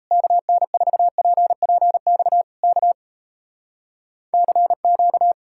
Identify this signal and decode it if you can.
Morse code, heard as a keyed tone, about 38 wpm, 700 Hz. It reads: KN4PPX K CQ